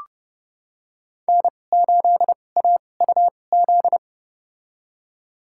Morse code, heard as a keyed tone, about 30 wpm, spaced as for 22 wpm, 700 Hz. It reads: N8AUZ
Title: USA Call Signs - 5 Characters - 2x Character Spacing - 30wpm